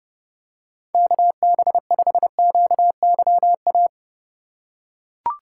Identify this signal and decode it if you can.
Morse code, heard as a keyed tone, about 30 wpm, 700 Hz. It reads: KB5QYA
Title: USA Call Signs - 6 Characters - 2x Character Spacing - 30wpm